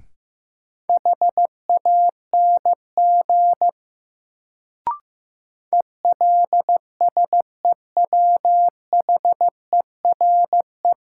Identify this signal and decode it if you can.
Morse code, heard as a keyed tone, about 15 wpm, 700 Hz. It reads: HANG ELSEWHERE
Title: Sentences from Top 1600 Words - New Word Drill - 15wpm